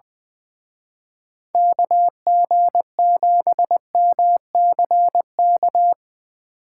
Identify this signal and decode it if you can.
Morse code, heard as a keyed tone, about 24 wpm, 700 Hz. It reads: KG7MCK
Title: Speed Racing - USA 2x3 Call Signs 20wpm